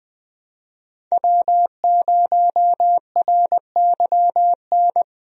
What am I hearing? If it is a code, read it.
Morse code, heard as a keyed tone, about 20 wpm, 700 Hz. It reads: W0RYN
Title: USA Call Signs - 5 Characters - 20wpm